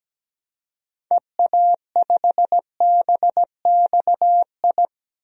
Morse code, heard as a keyed tone, about 17 wpm, 700 Hz. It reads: EA5BXI
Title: DXCC-100 Call Signs - 6 Characters - Repeated 3x - 17wpm